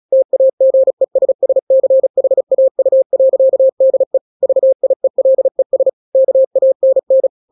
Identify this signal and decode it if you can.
Morse code, heard as a keyed tone, about 36 wpm, 530 Hz. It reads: TAGESSCHAU.DE VIELES KANN
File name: daily / 35wpm